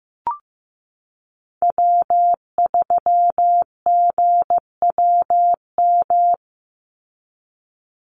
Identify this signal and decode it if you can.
Morse code, heard as a keyed tone, about 15 wpm, 700 Hz. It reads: W3GWM
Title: Call Signs Repeated 2x - 15wpm